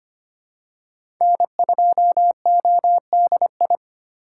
Morse code, heard as a keyed tone, about 25 wpm, 700 Hz. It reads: N2ODI
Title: USA Call Signs - 5 Characters - 2x Character Spacing - 25wpm